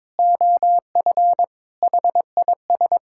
Morse code, heard as a keyed tone, about 22 wpm, 700 Hz. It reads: OF HIS
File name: practice / 22wpm / Sets of 6 Words - 22wpm